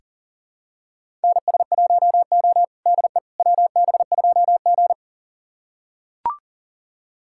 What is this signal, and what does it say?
Morse code, heard as a keyed tone, about 40 wpm, 700 Hz. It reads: NS1O DE WB2G
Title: USA Callsign DE Callsign - 4 Characters - 40wpm